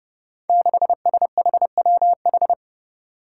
Morse code, heard as a keyed tone, about 32 wpm, 700 Hz. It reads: 6SHWH
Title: Five Challenging Letter-Number Combinations 30wpm